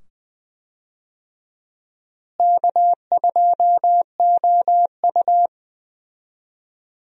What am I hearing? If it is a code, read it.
Morse code, heard as a keyed tone, about 21 wpm, 700 Hz. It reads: K2OU
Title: USA Call Signs - 4 Characters - Repeated 2x - 20wpm